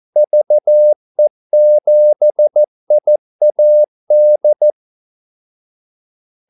Morse code, heard as a keyed tone, about 14 wpm, 600 Hz. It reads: VE7IAD
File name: Callsigns-14wpm